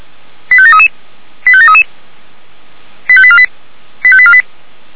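Five-tone selective call (ZVEI), heard as quick runs of five tones, tones of about 70 ms. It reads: 85733 85733 85848 85848
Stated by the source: ZVEI five-tone selective call 89240 received by SDR.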